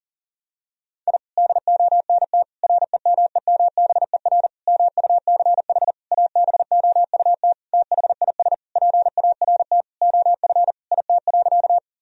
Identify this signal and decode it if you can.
Morse code, heard as a keyed tone, about 40 wpm, 700 Hz. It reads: I DONT REMEMBER MUCH ABOUT THIS PART OF IT.
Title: Sentences from Top 1000 Words - Review Part 1 - 40wpm